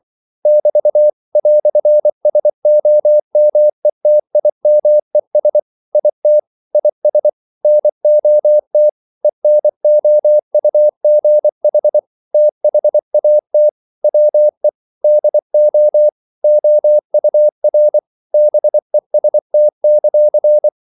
Morse code, heard as a keyed tone, about 24 wpm, 600 Hz. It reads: <BT> "SOMETIMES IT IS NOT ENOUGH THAT WE DO OUR BEST;
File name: CW / Practice / EnglishQuotations0424